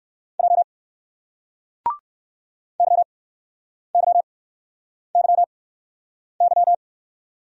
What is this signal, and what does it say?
Morse code, heard as a keyed tone, about 60 wpm, spaced as for 9 wpm, 700 Hz. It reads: J YYYY